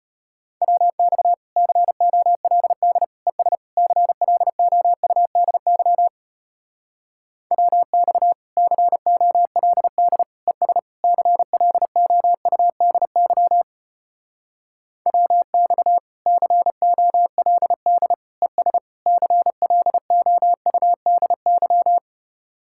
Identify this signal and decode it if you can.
Morse code, heard as a keyed tone, about 33 wpm, 700 Hz. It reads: WX COLD ES CLOUDY WX COLD ES CLOUDY WX COLD ES CLOUDY